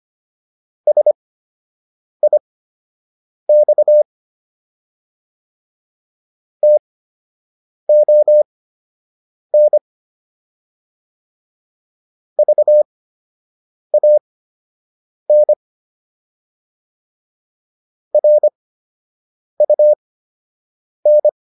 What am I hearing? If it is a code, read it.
Morse code, heard as a keyed tone, about 25 wpm, spaced as for 7 wpm, 600 Hz. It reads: SIX TON VAN RUN